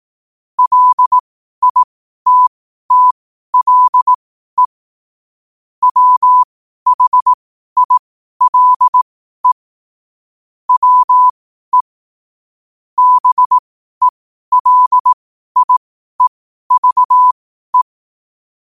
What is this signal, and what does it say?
Morse code, heard as a keyed tone, about 18 wpm, spaced as for 12 wpm, 1000 Hz. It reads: LITTLE WHILE WE BELIEVE
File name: qotdfiles / 12wpm